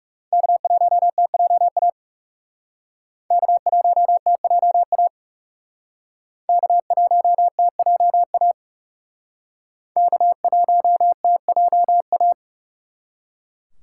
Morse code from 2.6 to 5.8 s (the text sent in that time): K1TJA